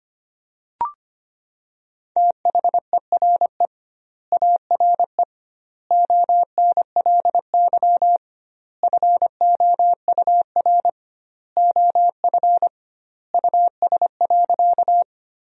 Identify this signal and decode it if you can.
Morse code, heard as a keyed tone, about 25 wpm, 700 Hz. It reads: THERE ARE ONLY FOUR OF US.
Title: 2x Word-Spacing - Sentences from Top 300 Words - 25wpm